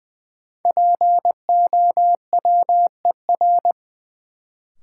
Morse code, heard as a keyed tone, about 21 wpm, 700 Hz. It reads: POWER
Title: Sets of 4 Words - New Word Drill - 20wpm